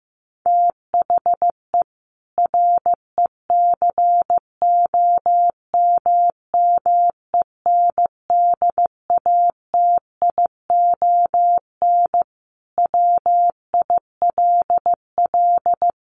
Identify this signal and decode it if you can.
Morse code, heard as a keyed tone, about 15 wpm, 700 Hz. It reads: THE RECOMMENDATION WILL
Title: Sentences from Top 1400 Words 15wpm